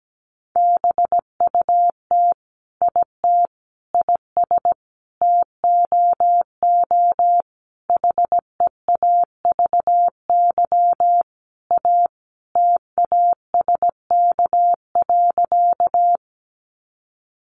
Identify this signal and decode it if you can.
Morse code, heard as a keyed tone, about 17 wpm, 700 Hz. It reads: BUT IT IS TOO HEAVY A TASK.
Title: Sentences from Top 1100 Words 17wpm